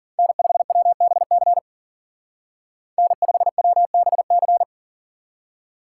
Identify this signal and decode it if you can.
Morse code, heard as a keyed tone, about 52 wpm, 700 Hz. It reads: N5WBC N5WBC